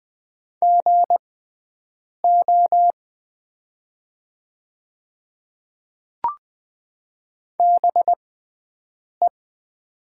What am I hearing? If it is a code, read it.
Morse code, heard as a keyed tone, about 20 wpm, spaced as for 7 wpm, 700 Hz. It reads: GO BE
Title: 6x Character Spacing - Top 300 Words - 20wpm